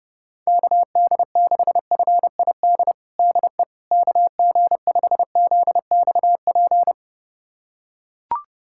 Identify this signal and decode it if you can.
Morse code, heard as a keyed tone, about 32 wpm, 700 Hz. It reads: KD6FID DE KG5ZXP
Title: USA Callsign DE Callsign - 6 Characters - 30wpm